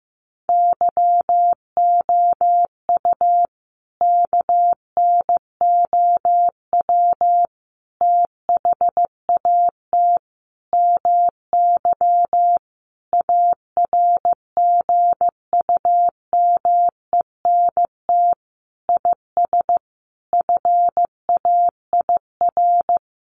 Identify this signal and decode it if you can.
Morse code, heard as a keyed tone, about 15 wpm, 700 Hz. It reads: YOU KNOW THAT MY ARGUMENT IS FAIR
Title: Sentences from Top 1400 Words 15wpm